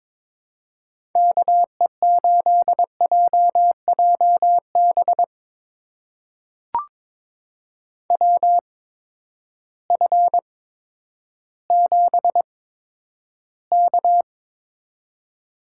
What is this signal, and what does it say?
Morse code, heard as a keyed tone, about 22 wpm, spaced as for 11 wpm, 700 Hz. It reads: KE8JJB WF7K
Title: USA Call Signs - 8x Character Spacing - 22wpm